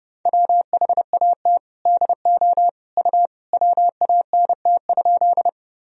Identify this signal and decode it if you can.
Morse code, heard as a keyed tone, about 30 wpm, 700 Hz. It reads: WHAT DO U WANT?
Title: Sentences from Top 200 Words and Abbreviations 30wpm